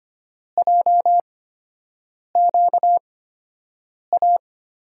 Morse code, heard as a keyed tone, about 25 wpm, spaced as for 7 wpm, 700 Hz. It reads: JQA
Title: USA Call Signs - 8x Character Spacing - 25wpm